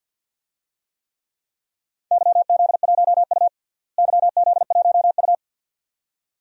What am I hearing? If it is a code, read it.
Morse code, heard as a keyed tone, about 50 wpm, 700 Hz. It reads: YZ1U YZ1U
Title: DXCC-100 Call Signs - 4 Characters - Repeated 2x - 50wpm